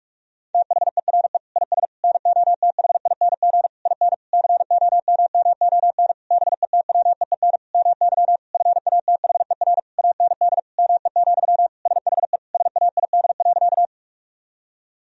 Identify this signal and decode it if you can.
Morse code, heard as a keyed tone, about 46 wpm, 700 Hz. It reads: THERE IS NOTHING IN COMMON BETWEEN MY FATHER AND ME, SHE SAID.
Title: Sentences from Top 1000 Words - Review Part 3 - 45wpm